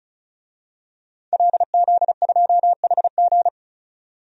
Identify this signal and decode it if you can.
Morse code, heard as a keyed tone, about 36 wpm, 700 Hz. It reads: LZ2HG
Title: DXCC-100 Call Signs - 5 Characters - 35wpm